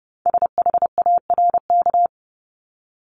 Morse code, heard as a keyed tone, about 30 wpm, 700 Hz. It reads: SHARK
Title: Multi-letter Phonemes Repeated 3x 30wpm